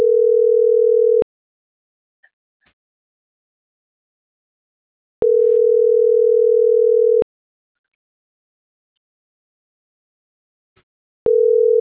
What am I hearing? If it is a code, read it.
Ringback tone (North America), 440 + 480 Hz.